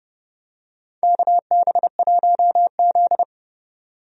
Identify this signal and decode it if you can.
Morse code, heard as a keyed tone, about 32 wpm, 700 Hz. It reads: KB1Z
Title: USA Call Signs - 4 Characters - 30wpm